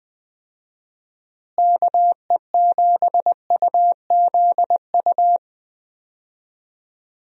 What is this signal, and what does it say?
Morse code, heard as a keyed tone, about 19 wpm, 700 Hz. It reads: KE7UZU
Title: USA Call Signs - 6 Characters - Repeated 3x - 20wpm